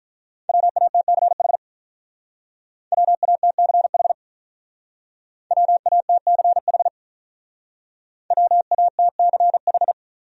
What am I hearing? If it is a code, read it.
Morse code, heard as a keyed tone, about 44 wpm, 700 Hz. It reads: WATCH WATCH WATCH WATCH